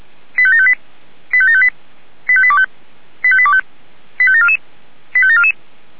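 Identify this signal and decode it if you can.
Five-tone selective call (ZVEI), heard as quick runs of five tones, tones of about 70 ms. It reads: 85758 85758 85825 85825 85744 85744
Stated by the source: ZVEI five-tone call received on an SDR.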